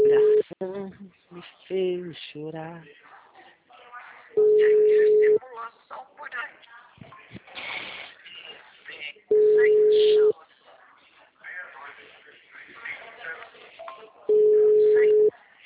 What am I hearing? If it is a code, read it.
Ringback tone (Europe, 425 Hz), with speech.